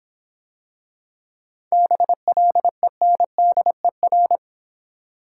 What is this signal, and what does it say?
Morse code, heard as a keyed tone, about 26 wpm, 700 Hz. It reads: BLENDER